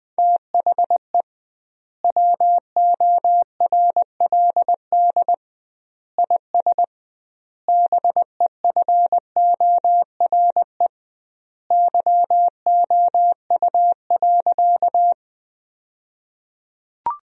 Morse code, heard as a keyed tone, about 20 wpm, 700 Hz. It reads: THE WORLD IS BEFORE YOU.
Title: 2x Word-Spacing - Sentences from Top 200 Words - 20wpm